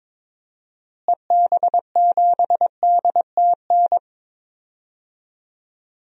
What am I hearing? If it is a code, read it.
Morse code, heard as a keyed tone, about 22 wpm, 700 Hz. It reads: EB7DTN
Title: DXCC-100 Call Signs - 6 Characters - Repeated 2x - 22wpm